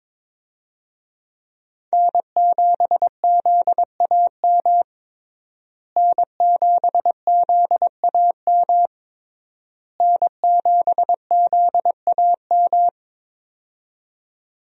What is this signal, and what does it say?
Morse code, heard as a keyed tone, about 21 wpm, 700 Hz. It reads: N7ZAM N7ZAM N7ZAM
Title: USA Call Signs - 5 Characters - Repeated 3x - 22wpm